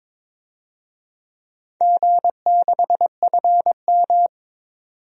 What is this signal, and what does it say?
Morse code, heard as a keyed tone, about 22 wpm, 700 Hz. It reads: G6FM
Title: DXCC-100 Call Signs - 4 Characters - 22wpm